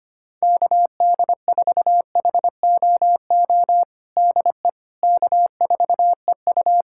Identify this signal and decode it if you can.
Morse code, heard as a keyed tone, about 25 wpm, 700 Hz. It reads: KD4HOO DE K4EU
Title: Practice QSO - Exchange 4 - Ending - 25wpm